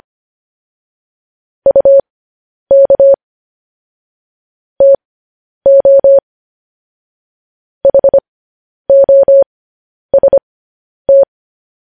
Morse code, heard as a keyed tone, about 25 wpm, spaced as for 10 wpm, 550 Hz. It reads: UK TO HOST
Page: news-headlines-in-morse-code-at-10-wpm